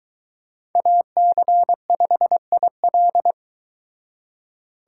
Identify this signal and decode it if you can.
Morse code, heard as a keyed tone, about 22 wpm, 700 Hz. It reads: AC5IL